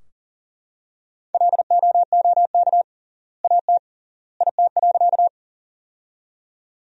Morse code, heard as a keyed tone, about 38 wpm, 700 Hz. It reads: LOOK AT IT.
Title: 3x Word-Spacing - Sentences from Top 100 Words - 40wpm